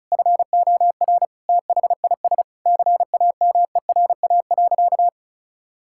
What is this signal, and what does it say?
Morse code, heard as a keyed tone, about 36 wpm, 700 Hz. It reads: FOR THIS CAMERA.
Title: Sentences from Top 1000 Words - Review Part 5 - 35wpm